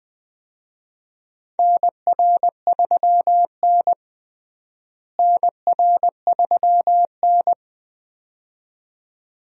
Morse code, heard as a keyed tone, about 20 wpm, 700 Hz. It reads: NR3N NR3N
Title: USA Call Signs - 4 Characters - Repeated 2x - 20wpm